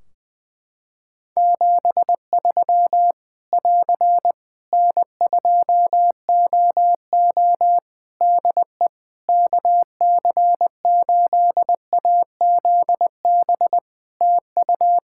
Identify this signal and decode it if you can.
Morse code, heard as a keyed tone, about 20 wpm, 700 Hz. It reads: 73 <AR> N2OO DE KC8AZB TU
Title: Practice QSO - Exchange 4 - Ending - 20wpm